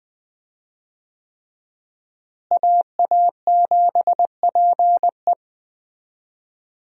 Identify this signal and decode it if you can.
Morse code, heard as a keyed tone, about 20 wpm, 700 Hz. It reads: AA7PE
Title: USA Call Signs - 5 Characters - Repeated 3x - 20wpm